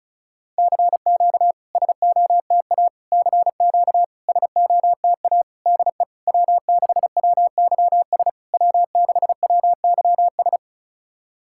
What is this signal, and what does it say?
Morse code, heard as a keyed tone, about 35 wpm, 660 Hz. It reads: CQ SOTA CQ SOTA DE W6WYS W6WYS
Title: SOTA QSO Practice 35wpm